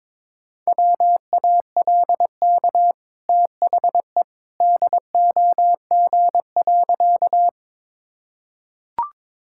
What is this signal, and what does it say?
Morse code, heard as a keyed tone, about 22 wpm, 690 Hz. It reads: WALK THE DOG.
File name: practice / 22wpm / Easy Sentences with Hints - 22wpm